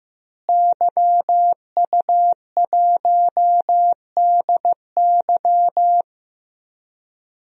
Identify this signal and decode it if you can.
Morse code, heard as a keyed tone, about 15 wpm, 700 Hz. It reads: YU1DY
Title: DXCC-100 Call Signs - 5 Characters - Repeated 3x - 15wpm